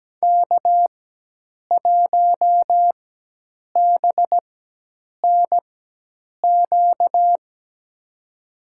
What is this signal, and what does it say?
Morse code, heard as a keyed tone, about 17 wpm, spaced as for 8 wpm, 700 Hz. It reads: K1BNQ
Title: USA Call Signs - 5 Characters - 4x Character Spacing - 17wpm